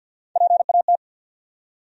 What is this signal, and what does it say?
Morse code, heard as a keyed tone, about 53 wpm, 700 Hz. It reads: PAT
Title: USA Common Names - Max 3 Letters - 50wpm